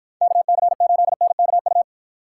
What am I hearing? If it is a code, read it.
Morse code, heard as a keyed tone, about 52 wpm, 700 Hz. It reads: KC8NKU